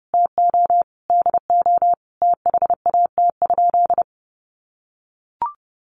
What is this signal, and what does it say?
Morse code, heard as a keyed tone, about 30 wpm, 700 Hz. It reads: TO DO THAT?
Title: Sentences from Top 900 Words 30wpm